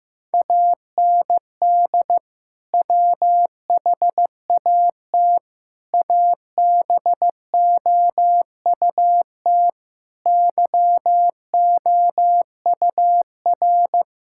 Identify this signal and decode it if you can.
Morse code, heard as a keyed tone, about 15 wpm, 700 Hz. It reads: AND WHAT ABOUT YOUR
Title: Sentences from Top 1000 Words - Review Part 6 - 15wpm